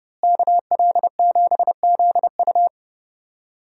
Morse code, heard as a keyed tone, about 30 wpm, 700 Hz. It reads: KL7ZU
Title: USA Call Signs - 5 Characters - Repeated 2x - 30wpm